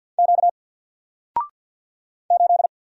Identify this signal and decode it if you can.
Morse code, heard as a keyed tone, about 49 wpm, 700 Hz. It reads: Q 8
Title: Single Letter-Number Alt 50wpm